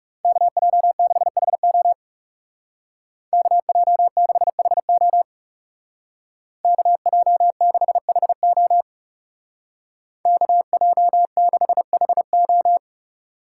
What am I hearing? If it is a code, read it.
Morse code, heard as a keyed tone, about 36 wpm, 700 Hz. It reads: KJ6HO KJ6HO KJ6HO KJ6HO
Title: Speed Racing - USA 2x2 Call Signs 30wpm